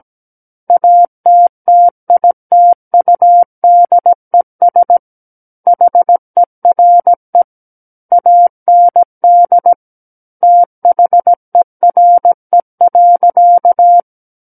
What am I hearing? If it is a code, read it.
Morse code, heard as a keyed tone, about 17 wpm, 700 Hz. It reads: ATTITUDES HERE AND THERE.
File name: Book One / 08-Chapter 8 - 17 WPM